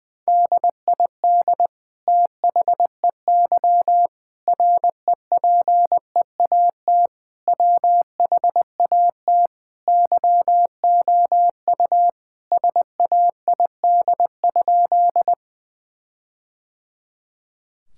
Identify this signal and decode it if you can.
Morse code, heard as a keyed tone, about 20 wpm, 700 Hz. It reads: DID THEY REPEAT WHAT YOU SAID?
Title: Sentences from Top 1600 Words 20wpm